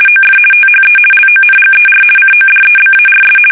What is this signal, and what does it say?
RTTY, about 50 baud. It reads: OF THE LONGER CODEBARS
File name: fsk850-50B / teletype-corp